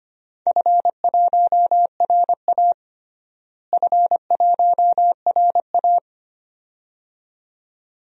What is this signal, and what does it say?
Morse code, heard as a keyed tone, about 25 wpm, 700 Hz. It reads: F1RA F1RA
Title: DXCC-100 Call Signs - 4 Characters - Repeated 2x - 25wpm